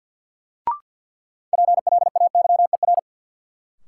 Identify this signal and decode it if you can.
Morse code, heard as a keyed tone, about 52 wpm, 700 Hz. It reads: PLAYER
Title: Sentences from Top 1000 Words - Review Part 4 - New Word Drill - 50wpm